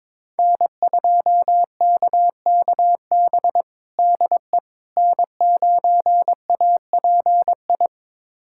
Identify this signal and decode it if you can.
Morse code, heard as a keyed tone, about 22 wpm, 700 Hz. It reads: N2KKB DE N9API
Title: USA Callsign DE Callsign - 5 Characters - 22wpm